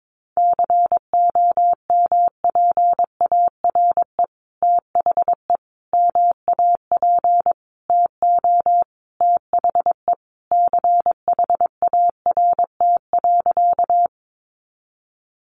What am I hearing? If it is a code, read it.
Morse code, heard as a keyed tone, about 22 wpm, 700 Hz. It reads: COMPARE THE MAP TO THE CHART.
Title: Sentences from Top 1300 Words 22wpm